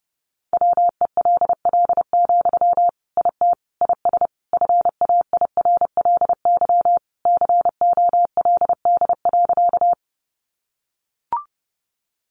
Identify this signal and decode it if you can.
Morse code, heard as a keyed tone, about 30 wpm, 700 Hz. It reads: WELL, IT IS FAIRLY COLD.
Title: Sentences from Top 1100 Words 30wpm